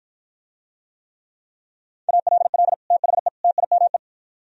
Morse code, heard as a keyed tone, about 52 wpm, 700 Hz. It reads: ALL THE TIME